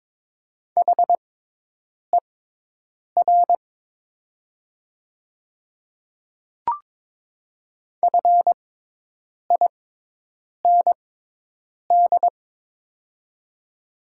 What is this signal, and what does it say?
Morse code, heard as a keyed tone, about 21 wpm, spaced as for 8 wpm, 700 Hz. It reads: HER FIND
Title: 6x Character Spacing - Top 100 Words - 22wpm